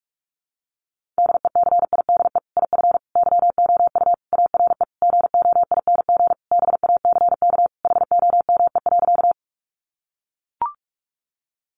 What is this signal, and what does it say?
Morse code, heard as a keyed tone, about 44 wpm, 700 Hz. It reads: DECIDE IF YOU ARE GOING BACK HOME.
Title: Sentences from Top 500 Words 45wpm